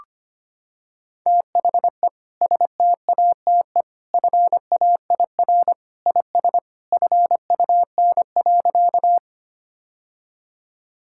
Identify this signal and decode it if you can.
Morse code, heard as a keyed tone, about 25 wpm, 700 Hz. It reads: THE STATE FAIR IS FUN.
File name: practice / 25wpm / Sentences with 2-Word Phrases - 25wpm